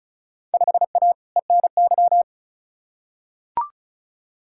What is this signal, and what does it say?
Morse code, heard as a keyed tone, about 34 wpm, 700 Hz. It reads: 5A ENY
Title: Field Day Exchanges - 35wpm